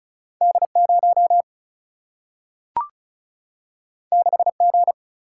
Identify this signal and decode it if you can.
Morse code, heard as a keyed tone, about 36 wpm, 700 Hz. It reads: D 0 6 G
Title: Single-Letter Single-Number 35wpm